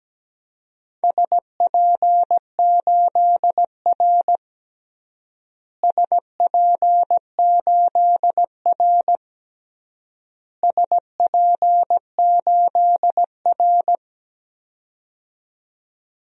Morse code, heard as a keyed tone, about 17 wpm, 700 Hz. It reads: SP8R SP8R SP8R
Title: DXCC-100 Call Signs - 4 Characters - Repeated 3x - 17wpm